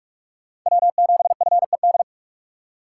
Morse code, heard as a keyed tone, about 42 wpm, 700 Hz. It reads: W7FED